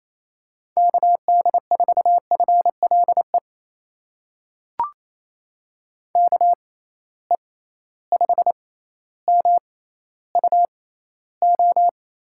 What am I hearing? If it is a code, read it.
Morse code, heard as a keyed tone, about 28 wpm, spaced as for 14 wpm, 700 Hz. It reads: KD4FLE KE5MUO
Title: USA Call Signs - 6 Characters - 6x Character Spacing - 28wpm